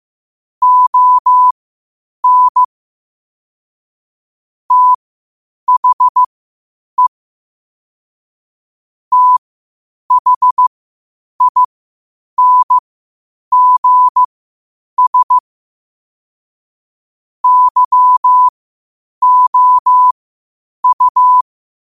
Morse code, heard as a keyed tone, about 15 wpm, spaced as for 8 wpm, 1000 Hz. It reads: ON THE THINGS YOU
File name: qotdfiles / 8wpm